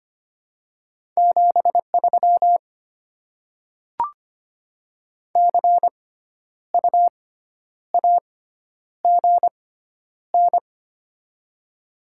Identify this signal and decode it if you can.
Morse code, heard as a keyed tone, about 25 wpm, spaced as for 10 wpm, 700 Hz. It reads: 73 CU AGN
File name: Top 20 QSO Elements - 6x Character Spacing - 25wpm